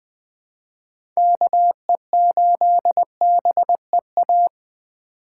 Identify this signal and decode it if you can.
Morse code, heard as a keyed tone, about 20 wpm, 700 Hz. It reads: KE8BEA
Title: USA Call Signs - 6 Characters - 3x Character Spacing - 20wpm